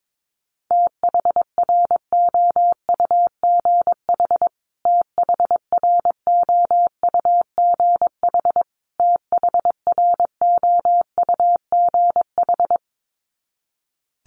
Morse code, heard as a keyed tone, about 22 wpm, 700 Hz. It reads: THROUGH THROUGH THROUGH
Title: Top 200 Words Repeated 22wpm